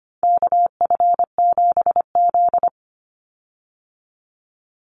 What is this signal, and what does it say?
Morse code, heard as a keyed tone, about 25 wpm, 700 Hz. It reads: KF7Z
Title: Call Signs Repeated 3x - 25wpm